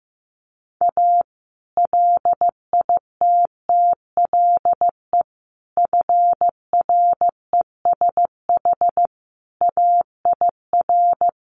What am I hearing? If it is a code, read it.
Morse code, heard as a keyed tone, about 15 wpm, 700 Hz. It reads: A LITTLE FRESH AIR
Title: Sentences from Top 1200 Words 15wpm